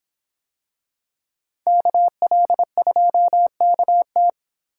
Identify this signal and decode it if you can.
Morse code, heard as a keyed tone, about 25 wpm, 700 Hz. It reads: KL2KT